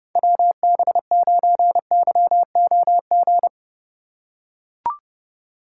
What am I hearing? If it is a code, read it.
Morse code, heard as a keyed tone, about 31 wpm, 700 Hz. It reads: WB9YOG
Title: USA Call Signs - 6 Characters - 3x Character Spacing - 30wpm